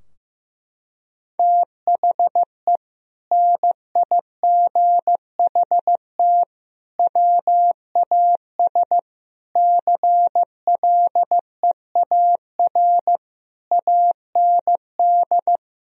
Morse code, heard as a keyed tone, about 15 wpm, 700 Hz. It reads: THE NIGHT WAS CLEAR AND
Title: Sentences from Top 1000 Words - Review Part 2 - 15wpm